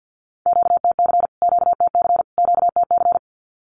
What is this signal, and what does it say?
Morse code, heard as a keyed tone, about 50 wpm, 700 Hz. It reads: QTC QTC QTC
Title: Q-Codes Repeated 3x 50wpm